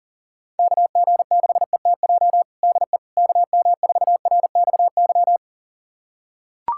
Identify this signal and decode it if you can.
Morse code, heard as a keyed tone, about 40 wpm, 700 Hz. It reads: KG6ETJ DE KM4RXY
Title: USA Callsign DE Callsign 40wpm